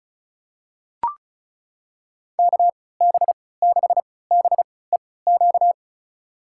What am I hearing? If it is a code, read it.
Morse code, heard as a keyed tone, about 36 wpm, spaced as for 20 wpm, 700 Hz. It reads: KB6BEQ
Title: USA Call Signs - 6 Characters - 3x Character Spacing - 35wpm